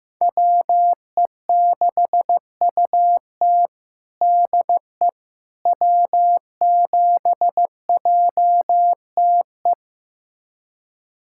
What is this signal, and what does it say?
Morse code, heard as a keyed tone, about 16 wpm, 700 Hz. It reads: WE6UT DE W7JTE
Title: USA Callsign DE Callsign - 5 Characters - 15wpm